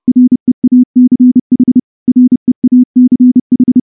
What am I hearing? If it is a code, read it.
Morse code, heard as a keyed tone, about 28 wpm, 250 Hz. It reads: REACH REACH
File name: morse-training-30WPM